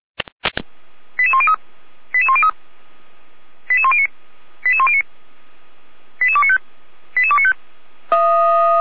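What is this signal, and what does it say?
Five-tone selective call (ZVEI), heard as quick runs of five tones, tones of about 70 ms. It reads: 88103 88103 88109 88109 88295 88295
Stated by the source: ZVEI five-tone selective call 88104 received by SDR, from the feuerwehr-orlamuende fire brigade audio archive.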